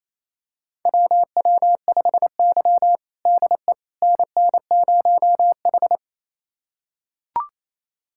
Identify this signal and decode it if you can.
Morse code, heard as a keyed tone, about 28 wpm, 700 Hz. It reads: WW5Y DE NN0H
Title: USA Callsign DE Callsign - 4 Characters - 28wpm